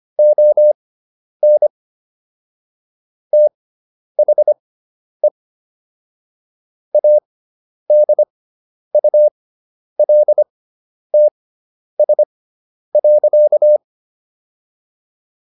Morse code, heard as a keyed tone, about 25 wpm, spaced as for 10 wpm, 600 Hz. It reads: ON THE ADULTS.